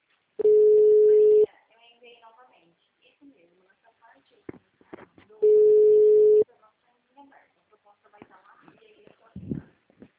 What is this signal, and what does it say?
Ringback tone (Europe, 425 Hz), with speech.
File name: out- -2006- - - .997